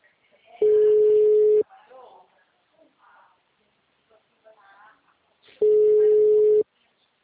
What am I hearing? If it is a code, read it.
Ringback tone (Europe, 425 Hz), with speech.
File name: out- -2041- - - .37773